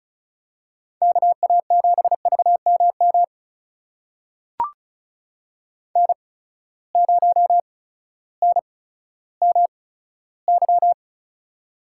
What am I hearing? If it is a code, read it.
Morse code, heard as a keyed tone, about 34 wpm, spaced as for 16 wpm, 700 Hz. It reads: KA7VMM N0NMY
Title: USA Call Signs - 8x Character Spacing - 35wpm